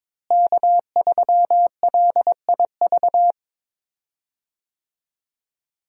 Morse code, heard as a keyed tone, about 22 wpm, 700 Hz. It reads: K3LIV
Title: USA Call Signs - 5 Characters - Repeated 3x - 22wpm